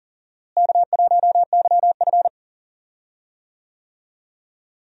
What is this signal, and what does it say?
Morse code, heard as a keyed tone, about 38 wpm, 700 Hz. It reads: K1YF
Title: USA Call Signs - 4 Characters - Repeated 3x - 40wpm